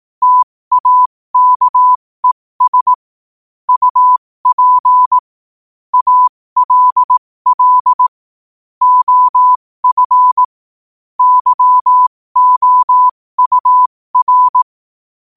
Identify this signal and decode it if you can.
Morse code, heard as a keyed tone, about 18 wpm, spaced as for 15 wpm, 1000 Hz. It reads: TAKES UP ALL OF YOUR
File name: qotdfiles / 15wpm